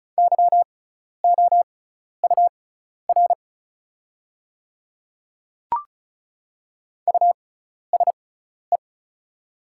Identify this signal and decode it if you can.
Morse code, heard as a keyed tone, about 36 wpm, spaced as for 12 wpm, 700 Hz. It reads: YOUR USE
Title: 6x Character Spacing - Top 100 Words - 35wpm